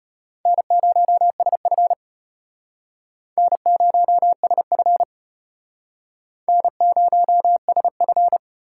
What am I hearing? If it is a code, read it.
Morse code, heard as a keyed tone, about 33 wpm, 700 Hz. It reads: N0SF N0SF N0SF